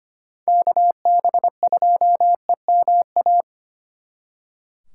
Morse code, heard as a keyed tone, about 33 wpm, 700 Hz. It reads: KB2EMA